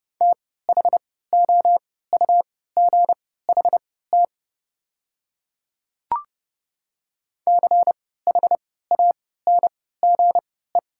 Morse code, heard as a keyed tone, about 30 wpm, spaced as for 17 wpm, 690 Hz. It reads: THOUGHT CHANGE
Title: 3x Character Spacing - Top 300 Words - 30wpm